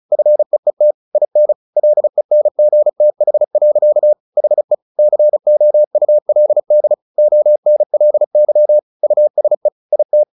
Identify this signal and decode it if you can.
Morse code, heard as a keyed tone, about 34 wpm, 600 Hz. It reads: FEET IN LENGTH. HE COULD ONLY USE IT
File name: the old man and the sea 2 35wpm